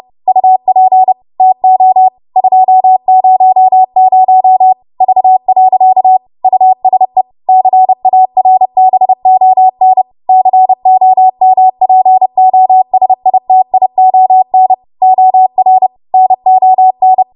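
Morse code, heard as a keyed tone, about 30 wpm, 750 Hz. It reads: UP TO 200 V. USE CARBON COMPOSITION OR NON
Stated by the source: W1AW Web Code Practice Files, 30 WPM run from ARRL; text from April 2023 QST.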